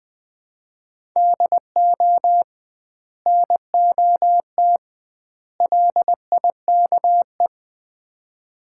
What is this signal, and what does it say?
Morse code, heard as a keyed tone, about 20 wpm, 700 Hz. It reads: DO NOT LIKE
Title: 2x Word-Spacing - Sets of 3 Words - 20wpm